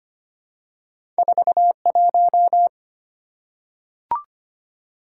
Morse code, heard as a keyed tone, about 25 wpm, 700 Hz. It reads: 41
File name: Two Letter-Number Combinations - 25wpm